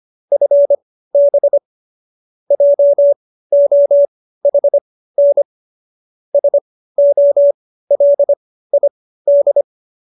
Morse code, heard as a keyed tone, about 25 wpm, spaced as for 15 wpm, 570 Hz. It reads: FB JOHN SOLID